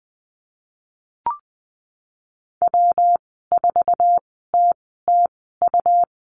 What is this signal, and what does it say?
Morse code, heard as a keyed tone, about 19 wpm, spaced as for 14 wpm, 700 Hz. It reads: W4TTU
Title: 147 Lesson 24g - Letter y - Call Signs 20wpm